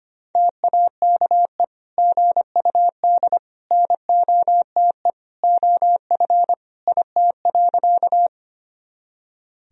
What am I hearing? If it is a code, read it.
Morse code, heard as a keyed tone, about 25 wpm, 700 Hz. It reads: TAKE GUD NOTE OF IT.
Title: Sentences from Top 500 Words and Abbreviations 25wpm